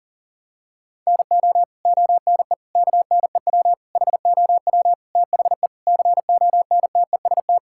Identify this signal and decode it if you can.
Morse code, heard as a keyed tone, about 37 wpm, 700 Hz. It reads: NO ONE KNEW HOW THE CONTEST
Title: Sentences from Top 1600 Words 40wpm